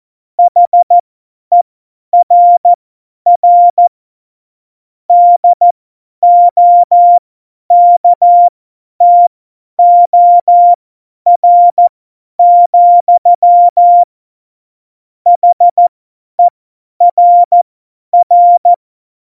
Morse code, heard as a keyed tone, about 14 wpm, spaced as for 10 wpm, 700 Hz. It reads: HERR DOKTOR, HERR
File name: Morsekurs / Witze / Witz 01. September 2020, 50 F70